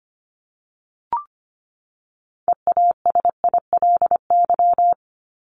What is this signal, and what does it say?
Morse code, heard as a keyed tone, about 25 wpm, 700 Hz. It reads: EASILY
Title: Top 751-1000 Words 25wpm